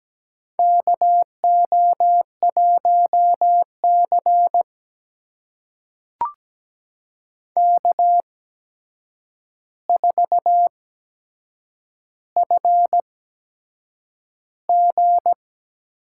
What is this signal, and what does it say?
Morse code, heard as a keyed tone, about 17 wpm, spaced as for 7 wpm, 700 Hz. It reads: KO1C K4FG
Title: USA Call Signs - 4 Characters - 8x Character Spacing - 17wpm